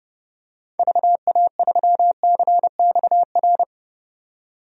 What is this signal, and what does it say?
Morse code, heard as a keyed tone, about 30 wpm, 700 Hz. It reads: VA3CXR
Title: DXCC-100 Call Signs - 6 Characters - 30wpm